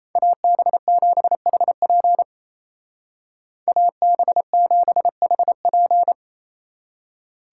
Morse code, heard as a keyed tone, about 34 wpm, 700 Hz. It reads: AB7HP AB7HP
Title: Speed Racing - USA 2x2 Call Signs 28wpm